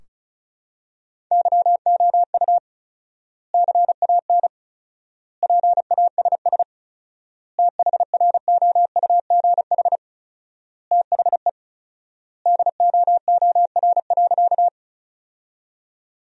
Morse code, heard as a keyed tone, about 36 wpm, 700 Hz. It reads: YOU CAN PASS THROUGH THE DOOR.
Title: 4x Word-Spacing - Sentences from Top 400 Words - 35wpm